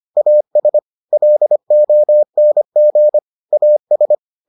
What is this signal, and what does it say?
Morse code, heard as a keyed tone, about 25 wpm, 600 Hz. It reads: AS LONG AS